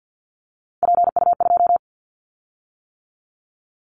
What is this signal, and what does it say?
Morse code, heard as a keyed tone, about 50 wpm, 700 Hz. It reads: PU2
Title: Callsign Prefixes for Top 100 DXCC - 50wpm